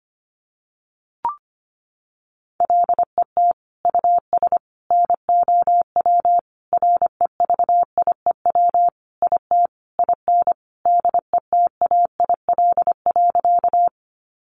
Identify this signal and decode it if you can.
Morse code, heard as a keyed tone, about 25 wpm, 700 Hz. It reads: LET US NOW REVIEW IT IN DETAIL.
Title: Sentences from Top 1500 Words 25wpm